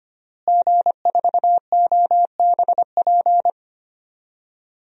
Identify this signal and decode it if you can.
Morse code, heard as a keyed tone, about 25 wpm, 700 Hz. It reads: G4OBP
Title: DXCC-100 Call Signs - 5 Characters - Repeated 2x - 25wpm